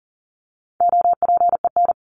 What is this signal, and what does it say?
Morse code, heard as a keyed tone, about 40 wpm, 700 Hz. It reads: OPEN
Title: Top 300 Words 40wpm